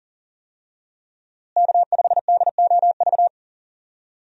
Morse code, heard as a keyed tone, about 40 wpm, 700 Hz. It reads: K5DOV
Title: USA Call Signs - 5 Characters - Repeated 3x - 40wpm